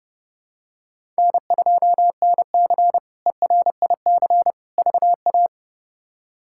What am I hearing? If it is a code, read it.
Morse code, heard as a keyed tone, about 30 wpm, 700 Hz. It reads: N2NC ERIC VA
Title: K1USN SST QSO Practice - 30wpm